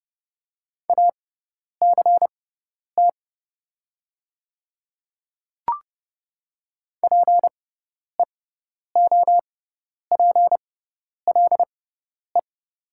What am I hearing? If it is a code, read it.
Morse code, heard as a keyed tone, about 30 wpm, spaced as for 10 wpm, 700 Hz. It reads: ACT PEOPLE
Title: 6x Character Spacing - Top 200 Words - 30wpm